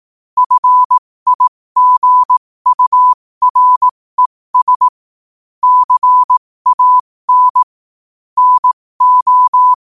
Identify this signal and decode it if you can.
Morse code, heard as a keyed tone, about 18 wpm, spaced as for 15 wpm, 1000 Hz. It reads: FIGURES CAN NO
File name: qotdfiles / 15wpm